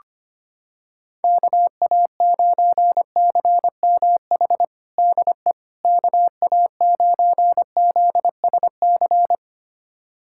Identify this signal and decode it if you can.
Morse code, heard as a keyed tone, about 25 wpm, 700 Hz. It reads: KA9CMH DE KA9ZSC
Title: USA Callsign DE Callsign 25wpm